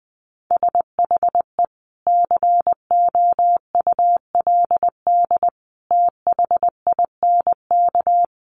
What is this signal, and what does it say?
Morse code, heard as a keyed tone, about 20 wpm, 700 Hz. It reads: SHE COULD THINK
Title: Sentences from Top 500 Words 20wpm